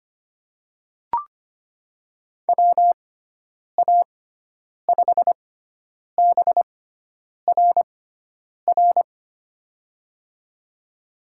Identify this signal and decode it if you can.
Morse code, heard as a keyed tone, about 25 wpm, spaced as for 9 wpm, 700 Hz. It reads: WA5BRR
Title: USA Call Signs - 6 Characters - 6x Character Spacing - 25wpm